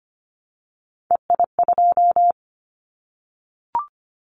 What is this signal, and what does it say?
Morse code, heard as a keyed tone, about 25 wpm, 690 Hz. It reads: EI2
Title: Callsign Prefixes for Top 50 DXCC - 25wpm